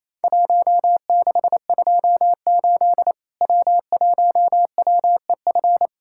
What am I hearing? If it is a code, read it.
Morse code, heard as a keyed tone, about 28 wpm, 700 Hz. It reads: 1628 W1WEF
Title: Worked All Europe DX Contest - QTC Exchange - 28wpm